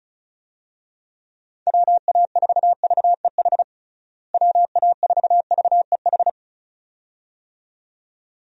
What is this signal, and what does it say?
Morse code, heard as a keyed tone, about 34 wpm, 700 Hz. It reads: WA4VEH WA4VEH
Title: USA Call Signs - 6 Characters - Repeated 2x - 35wpm